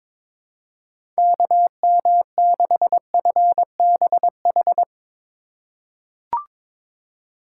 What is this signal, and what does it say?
Morse code, heard as a keyed tone, about 22 wpm, 700 Hz. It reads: KM6FBH
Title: USA Call Signs - 4x Character Spacing - 22wpm